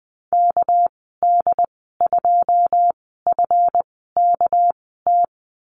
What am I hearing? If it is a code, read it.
Morse code, heard as a keyed tone, about 20 wpm, spaced as for 15 wpm, 700 Hz. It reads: KD2FKT
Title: 207 Lesson 33g - Letter j - Call Signs 20wpm